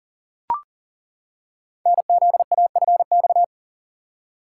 Morse code, heard as a keyed tone, about 38 wpm, 700 Hz. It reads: NZAFX
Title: Code Groups - 5 Letters - 40wpm